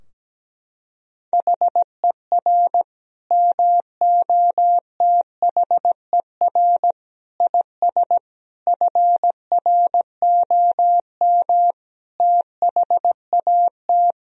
Morse code, heard as a keyed tone, about 17 wpm, 700 Hz. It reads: HER MOTHER IS FROM THAT
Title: Sentences from Top 1000 Words - Review Part 2 - 17wpm